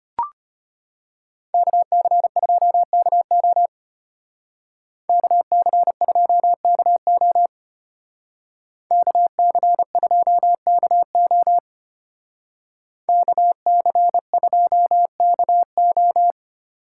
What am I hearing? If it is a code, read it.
Morse code, heard as a keyed tone, about 33 wpm, 700 Hz. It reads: KC2KO KC2KO KC2KO KC2KO